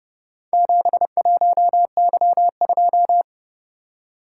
Morse code, heard as a keyed tone, about 30 wpm, 700 Hz. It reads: 71Y2